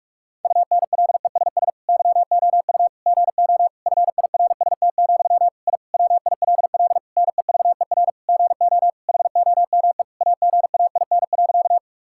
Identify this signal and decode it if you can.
Morse code, heard as a keyed tone, about 46 wpm, 700 Hz. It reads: UNLESS YOU GO FIRST, I WILL NEVER GO HOME AGAIN.
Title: Sentences from Top 1000 Words - Review Part 6 - 45wpm